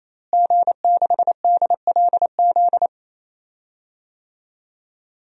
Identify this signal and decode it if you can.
Morse code, heard as a keyed tone, about 28 wpm, 700 Hz. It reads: G6DLZ
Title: DXCC-100 Call Signs - 5 Characters - Repeated 3x - 28wpm